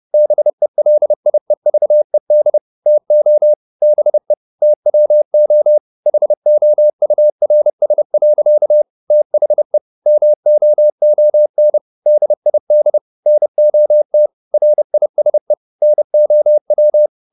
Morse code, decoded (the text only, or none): BELIEVED TO BE TWO HOURS. THE MOON DID NOT RISE NOW